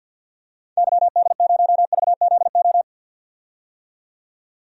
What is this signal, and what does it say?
Morse code, heard as a keyed tone, about 46 wpm, 700 Hz. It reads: YD0VZO
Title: DXCC-100 Call Signs - 6 Characters - 50wpm